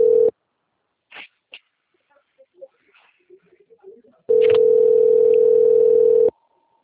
Ringback tone (North America), 440 + 480 Hz.
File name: out- - - - - .46769